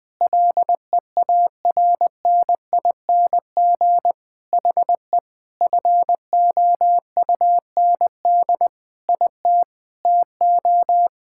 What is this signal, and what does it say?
Morse code, heard as a keyed tone, about 20 wpm, 700 Hz. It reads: LEARNING HE FOUND IT TO
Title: Sentences with 4-Word Phrases - 20wpm